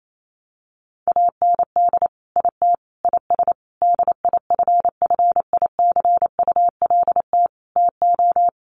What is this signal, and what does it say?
Morse code, heard as a keyed tone, about 28 wpm, 700 Hz. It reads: AND IT IS DIFFICULT TO
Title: Sentences from Top 1100 Words 28wpm